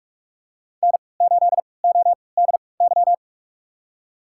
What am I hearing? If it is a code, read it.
Morse code, heard as a keyed tone, about 46 wpm, spaced as for 26 wpm, 700 Hz. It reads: N8ODY
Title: USA Call Signs - 5 Characters - 3x Character Spacing - 45wpm